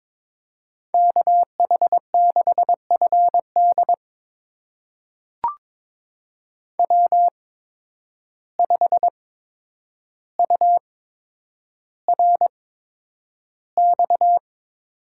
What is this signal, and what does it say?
Morse code, heard as a keyed tone, about 22 wpm, spaced as for 9 wpm, 700 Hz. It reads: KH6FD W5URX
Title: USA Call Signs - 5 Characters - 8x Character Spacing - 22wpm